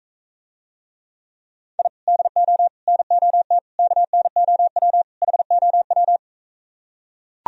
Morse code, heard as a keyed tone, about 43 wpm, 700 Hz. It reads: I DO NOT KNOW HOW